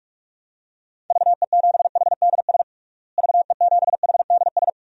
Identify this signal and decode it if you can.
Morse code, heard as a keyed tone, about 41 wpm, 700 Hz. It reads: VE7HDS VE7HDS
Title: DXCC-100 Call Signs - 6 Characters - Repeated 2x - 45wpm